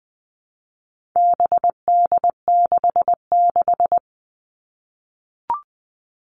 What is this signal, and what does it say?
Morse code, heard as a keyed tone, about 20 wpm, 700 Hz. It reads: BD66
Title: All Cut Numbers - 4-Digit 20wpm